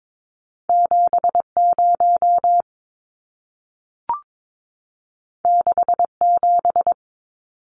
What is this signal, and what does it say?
Morse code, heard as a keyed tone, about 22 wpm, 700 Hz. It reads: 70 67
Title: Numbers - 2-Digit 22wpm